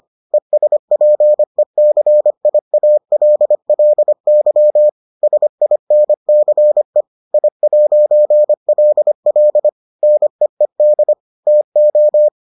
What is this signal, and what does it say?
Morse code, heard as a keyed tone, about 25 wpm, 600 Hz. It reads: ESPECIALLY SINCE I'LL NEED TO